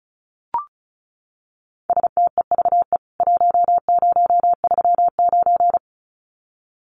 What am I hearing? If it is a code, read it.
Morse code, heard as a keyed tone, about 34 wpm, 700 Hz. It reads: STEVE 1039
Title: CWT Exchange 35wpm